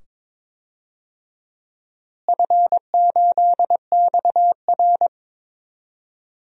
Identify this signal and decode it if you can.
Morse code, heard as a keyed tone, about 22 wpm, 700 Hz. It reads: F8XR
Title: DXCC-100 Call Signs - 4 Characters - Repeated 2x - 22wpm